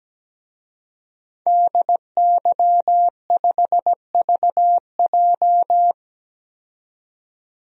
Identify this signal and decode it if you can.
Morse code, heard as a keyed tone, about 17 wpm, 700 Hz. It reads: DY5VJ
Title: DXCC-100 Call Signs - 5 Characters - 17wpm